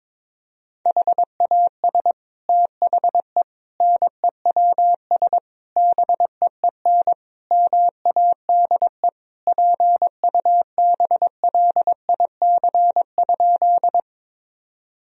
Morse code, heard as a keyed tone, about 22 wpm, 700 Hz. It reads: HAS THE NEWS BEEN MADE PUBLIC?
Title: Sentences from Top 1000 Words - Review Part 3 - 22wpm